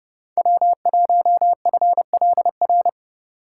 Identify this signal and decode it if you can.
Morse code, heard as a keyed tone, about 31 wpm, 700 Hz. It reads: W1FLR
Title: USA Call Signs - 5 Characters - 4x Character Spacing - 30wpm